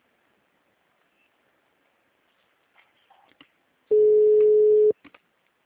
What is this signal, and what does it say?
Ringback tone (Europe, 425 Hz).